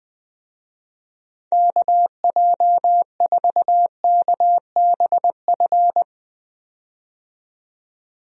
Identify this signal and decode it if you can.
Morse code, heard as a keyed tone, about 20 wpm, 700 Hz. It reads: KJ4KBF
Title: USA Call Signs - 6 Characters - 20wpm